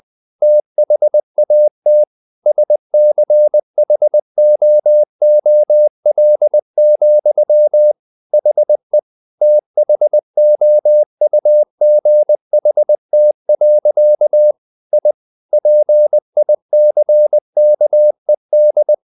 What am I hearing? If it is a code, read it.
Morse code, heard as a keyed tone, about 20 wpm, 600 Hz. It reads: THAT SCHOOL, HE THOUGHT. I PICKED
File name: the old man and the sea / the old man and the sea part 1 20wpm